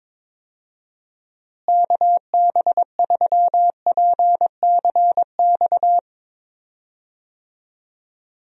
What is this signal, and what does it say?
Morse code, heard as a keyed tone, about 22 wpm, 700 Hz. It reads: KB3PCX
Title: USA Call Signs - 6 Characters - Repeated 3x - 22wpm